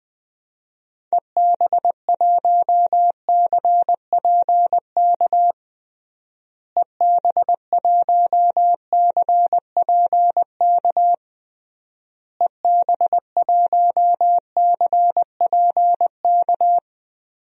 Morse code, heard as a keyed tone, about 20 wpm, 700 Hz. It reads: EB1CPK EB1CPK EB1CPK
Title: DXCC-100 Call Signs - 6 Characters - Repeated 3x - 20wpm